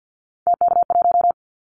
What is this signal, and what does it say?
Morse code, heard as a keyed tone, about 50 wpm, 690 Hz. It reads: TK1
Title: Callsign Prefixes for Top 100 DXCC - 50wpm